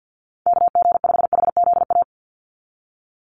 Morse code, heard as a keyed tone, about 50 wpm, 700 Hz. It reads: KG5HZA
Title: USA Call Signs 50wpm